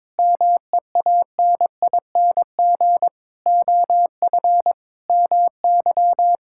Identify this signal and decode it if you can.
Morse code, heard as a keyed tone, about 22 wpm, 700 Hz. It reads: MEANING OF MY
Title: Sentences from Top 1000 Words - Review Part 6 - 22wpm